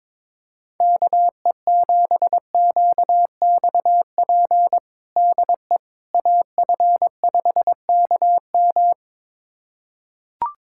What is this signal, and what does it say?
Morse code, heard as a keyed tone, about 22 wpm, 700 Hz. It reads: KE7QXP DE AF5KM
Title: USA Callsign DE Callsign 22wpm